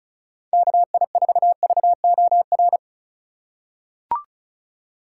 Morse code, heard as a keyed tone, about 36 wpm, 700 Hz. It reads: KI4VOR
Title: USA Call Signs - 6 Characters - 3x Character Spacing - 35wpm